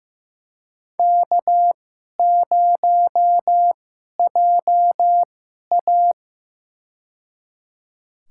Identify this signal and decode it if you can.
Morse code, heard as a keyed tone, about 15 wpm, spaced as for 11 wpm, 700 Hz. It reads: K0JA
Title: USA Call Signs - 2x Character Spacing - 15wpm